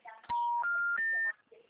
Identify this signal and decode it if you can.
Special information tone.
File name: out- -2009